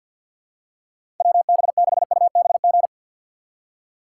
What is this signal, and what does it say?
Morse code, heard as a keyed tone, about 48 wpm, 690 Hz. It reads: WB6UBG
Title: USA Call Signs - 6 Characters - 8x Character Spacing - 50wpm